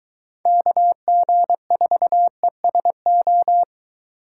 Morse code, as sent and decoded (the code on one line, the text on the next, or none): -.- --. ....- . ... ---
KG4ESO